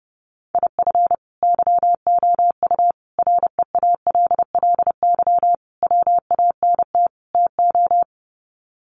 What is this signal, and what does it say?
Morse code, heard as a keyed tone, about 30 wpm, 700 Hz. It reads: IF YOU REALLY WANT TO
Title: Sets of 5 Words Encore2 30wpm